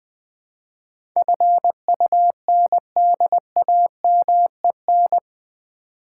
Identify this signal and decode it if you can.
Morse code, heard as a keyed tone, about 20 wpm, 700 Hz. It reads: FUNDAMEN
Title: Build A Word - 4 Letters At A Time - 11-Letter Words - 20wpm